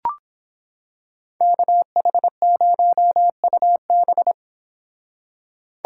Morse code, heard as a keyed tone, about 26 wpm, 700 Hz. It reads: KH0UB